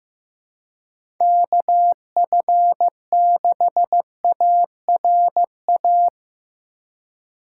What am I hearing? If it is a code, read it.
Morse code, heard as a keyed tone, about 16 wpm, 700 Hz. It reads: KF6ARA
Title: USA Call Signs - 6 Characters - 4x Character Spacing - 15wpm